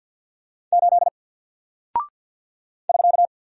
Morse code, heard as a keyed tone, about 52 wpm, 700 Hz. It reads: 8 <SK>
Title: Single Character (Letter:Number:Punctuation:Prosign) Alt 50wpm